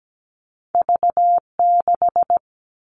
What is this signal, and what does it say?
Morse code, heard as a keyed tone, about 17 wpm, 700 Hz. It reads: V6
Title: All Cut Numbers - 2-Digit 17wpm